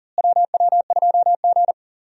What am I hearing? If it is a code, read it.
Morse code, heard as a keyed tone, about 37 wpm, 700 Hz. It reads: WW2G
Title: USA Call Signs - 4 Characters - Repeated 2x - 40wpm